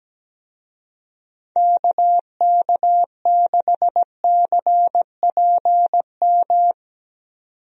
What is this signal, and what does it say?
Morse code, heard as a keyed tone, about 17 wpm, 700 Hz. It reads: KK6CPM
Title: USA Call Signs - 6 Characters - 17wpm